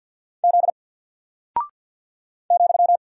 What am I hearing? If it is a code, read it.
Morse code, heard as a keyed tone, about 48 wpm, 700 Hz. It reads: Z ,
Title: Single Character (Letter:Number:Punctuation:Prosign) Alt 50wpm